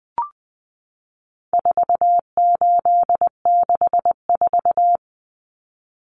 Morse code, heard as a keyed tone, about 20 wpm, 700 Hz. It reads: 4864
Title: Numbers - 4-Digit 20wpm